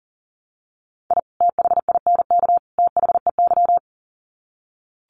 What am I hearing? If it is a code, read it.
Morse code, heard as a keyed tone, about 40 wpm, 700 Hz. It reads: I THINK THEY
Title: Sets of 3 Words 40wpm